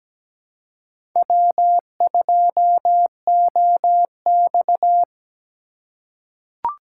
Morse code, heard as a keyed tone, about 17 wpm, 700 Hz. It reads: W2OX
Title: USA Call Signs - 4 Characters - 3x Character Spacing - 17wpm